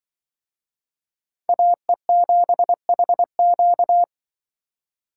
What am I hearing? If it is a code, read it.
Morse code, heard as a keyed tone, about 25 wpm, 700 Hz. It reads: AE7HQ
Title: Speed Racing - USA 2x2 Call Signs 20wpm